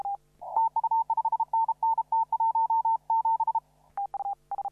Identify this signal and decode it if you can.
Morse code, heard as a keyed tone, about 32 wpm, 900 Hz. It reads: TU 5NN T17
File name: CW / RHOB CW 2017 HA6P 0001